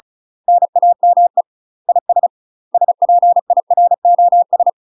Morse code, heard as a keyed tone, about 36 wpm, 700 Hz. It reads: NAME IS SPIROS
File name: UK MASTER QSOs TRAINING/35WPM